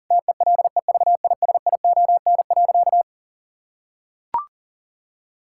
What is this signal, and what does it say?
Morse code, heard as a keyed tone, about 40 wpm, 700 Hz. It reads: TELEVISION.
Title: Sentences from Top 1000 Words - Review Part 5 - 40wpm